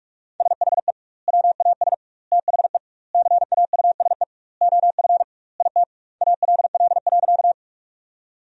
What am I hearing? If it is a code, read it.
Morse code, heard as a keyed tone, about 41 wpm, 700 Hz. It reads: SHE WAS THE CAUSE OF IT ALL.
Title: 2x Word-Spacing - Sentences from Top 200 Words - 45wpm